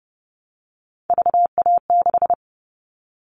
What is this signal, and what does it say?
Morse code, heard as a keyed tone, about 30 wpm, 700 Hz. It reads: VA6
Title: Callsign Prefixes for Top 50 DXCC - 30wpm